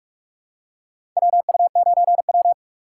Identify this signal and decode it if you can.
Morse code, heard as a keyed tone, about 46 wpm, 700 Hz. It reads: WU9W
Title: USA Call Signs - 4 Characters - 3x Character Spacing - 45wpm